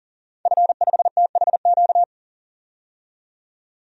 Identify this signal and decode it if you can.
Morse code, heard as a keyed tone, about 38 wpm, 700 Hz. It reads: F5THQ
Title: DXCC-100 Call Signs - 5 Characters - 40wpm